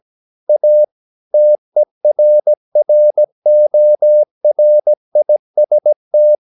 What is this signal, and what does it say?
Morse code, heard as a keyed tone, about 17 wpm, 600 Hz. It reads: A TERRORIST
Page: Romanen Never af Ken Follet på engelsk.